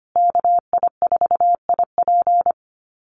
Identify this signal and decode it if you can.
Morse code, heard as a keyed tone, about 25 wpm, 700 Hz. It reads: KI4IP
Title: USA Call Signs Repeated 3x 25wpm